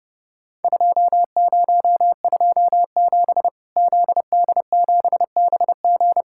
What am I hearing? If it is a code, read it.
Morse code, heard as a keyed tone, about 30 wpm, 700 Hz. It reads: 2027 ZD7BG
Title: Worked All Europe DX Contest - QTC Exchange - 30wpm